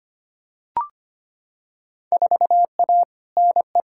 Morse code, heard as a keyed tone, about 25 wpm, 700 Hz. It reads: 4A NE
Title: Field Day Exchanges - 25wpm